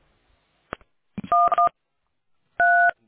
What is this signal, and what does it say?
DTMF, heard as telephone touch tones, tones of about 180 ms. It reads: 113